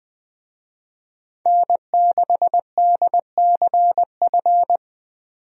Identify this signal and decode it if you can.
Morse code, heard as a keyed tone, about 20 wpm, 700 Hz. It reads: N6DCF
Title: USA Call Signs - 5 Characters - 20wpm